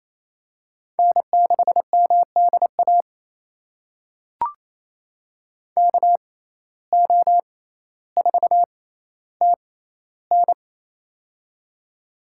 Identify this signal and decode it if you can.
Morse code, heard as a keyed tone, about 28 wpm, spaced as for 14 wpm, 700 Hz. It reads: N6MDA KO4TN
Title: USA Call Signs - 6x Character Spacing - 28wpm